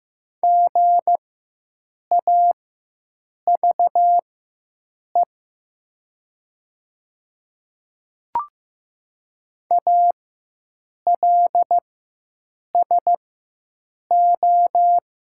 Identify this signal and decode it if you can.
Morse code, heard as a keyed tone, about 15 wpm, spaced as for 7 wpm, 700 Hz. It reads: GAVE ALSO
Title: 4x Character Spacing - Top 500 Words - 15wpm